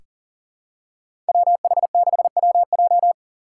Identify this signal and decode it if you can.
Morse code, heard as a keyed tone, about 40 wpm, 700 Hz. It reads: WH6WJ
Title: USA Call Signs - 5 Characters - 2x Character Spacing - 40wpm